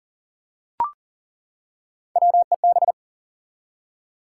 Morse code, heard as a keyed tone, about 39 wpm, 700 Hz. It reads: WEB
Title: Practice QSO - Exchange 3 - More Info - 40wpm